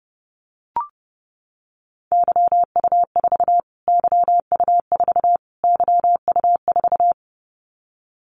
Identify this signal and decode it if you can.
Morse code, heard as a keyed tone, about 30 wpm, 700 Hz. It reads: YU4 YU4 YU4
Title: Callsign Prefixes for Top 25 DXCC - Repeated 3x - 30wpm